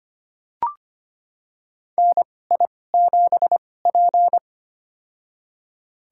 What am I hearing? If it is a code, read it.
Morse code, heard as a keyed tone, about 25 wpm, spaced as for 18 wpm, 700 Hz. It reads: NI7P
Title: USA Call Signs - 4 Characters - 2x Character Spacing - 25wpm